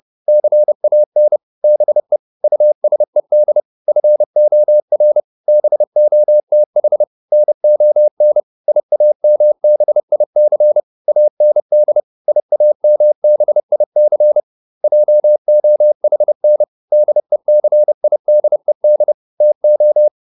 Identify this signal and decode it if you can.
Morse code, heard as a keyed tone, about 30 wpm, 600 Hz. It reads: CAN BE USED FOR BOTH NON IAMBIC AND IAMBIC JOHN DECIDED TO
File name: SS301 30